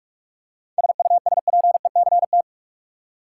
Morse code, heard as a keyed tone, about 43 wpm, 700 Hz. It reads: SUSPECT
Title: Seven Letter Words - 45wpm